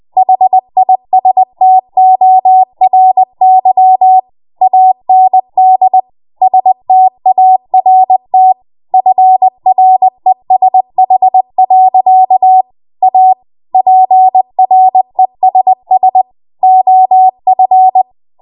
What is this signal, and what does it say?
Morse code, heard as a keyed tone, about 20 wpm, 750 Hz. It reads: HISTORY AND START FRESH. A PRESS OF
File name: Morse / 20WPM